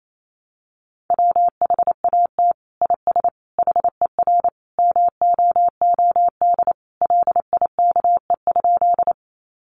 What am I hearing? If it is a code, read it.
Morse code, heard as a keyed tone, about 28 wpm, 700 Hz. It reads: WHAT IS HER MOOD LIKE?
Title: Sentences from Top 1100 Words 28wpm